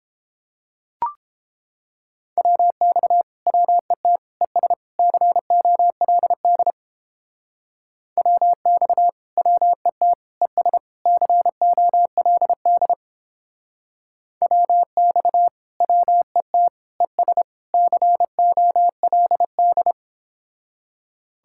Morse code, heard as a keyed tone, about 31 wpm, 700 Hz. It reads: WX WET ES COLD WX WET ES COLD WX WET ES COLD